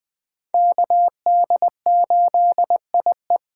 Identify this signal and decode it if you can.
Morse code, heard as a keyed tone, about 20 wpm, 700 Hz. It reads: KD8IE
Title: 2023 POTA Activator Call Signs - 20wpm